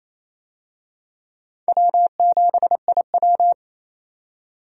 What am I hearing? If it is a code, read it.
Morse code, heard as a keyed tone, about 27 wpm, 700 Hz. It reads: W7IW
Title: USA Call Signs - 4 Characters - 28wpm